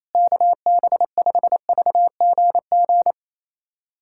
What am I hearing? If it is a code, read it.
Morse code, heard as a keyed tone, about 28 wpm, 700 Hz. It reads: KB5VGG
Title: USA Callsign DE Callsign - 6 Characters - 28wpm